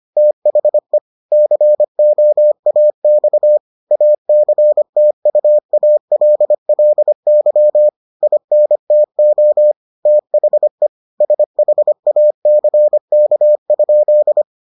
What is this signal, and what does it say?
Morse code, heard as a keyed tone, about 25 wpm, 600 Hz. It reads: THE COAX ACTUALLY INTO THE SHACK?